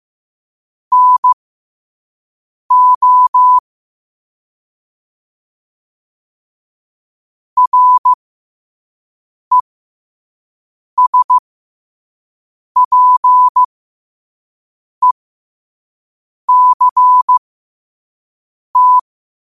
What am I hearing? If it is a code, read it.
Morse code, heard as a keyed tone, about 15 wpm, spaced as for 5 wpm, 1000 Hz. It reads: NO RESPECT
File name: qotdfiles / 5wpm